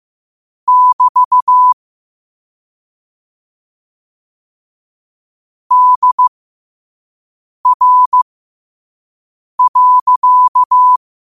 Morse code, heard as a keyed tone, about 15 wpm, spaced as for 5 wpm, 1000 Hz. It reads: <BT> DR.
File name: qotdfiles / 5wpm